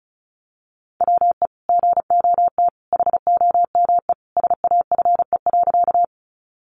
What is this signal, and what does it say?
Morse code, heard as a keyed tone, about 34 wpm, 700 Hz. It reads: WE GOT HOME SAFE.
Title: Sentences from Top 900 Words 35wpm